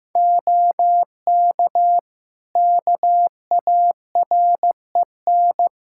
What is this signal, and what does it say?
Morse code, heard as a keyed tone, about 15 wpm, 700 Hz. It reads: OK KAREN
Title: Practice QSO - Exchange 2 - Info - 15wpm